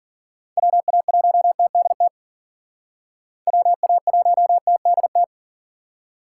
Morse code, heard as a keyed tone, about 51 wpm, 700 Hz. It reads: WA1TDT WA1TDT